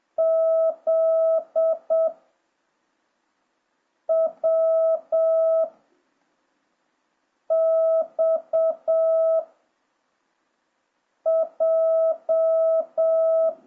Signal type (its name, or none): Morse code